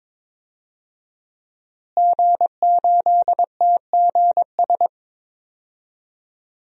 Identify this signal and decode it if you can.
Morse code, heard as a keyed tone, about 22 wpm, 700 Hz. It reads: G8TGS
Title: DXCC-100 Call Signs - 5 Characters - Repeated 2x - 22wpm